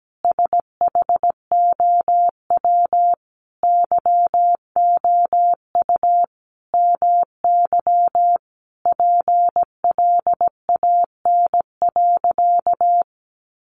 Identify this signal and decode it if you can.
Morse code, heard as a keyed tone, about 17 wpm, 700 Hz. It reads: SHOW YOU MY PLAN.
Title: Sentences from Top 500 Words 17wpm